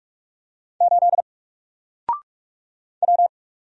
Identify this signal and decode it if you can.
Morse code, heard as a keyed tone, about 46 wpm, 700 Hz. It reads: 8 W
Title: Single Letter-Number Alt 45wpm